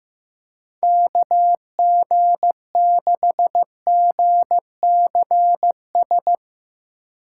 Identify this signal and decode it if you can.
Morse code, heard as a keyed tone, about 15 wpm, 700 Hz. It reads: KG6GCS
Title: USA Call Signs - 6 Characters - 6x Character Spacing - 15wpm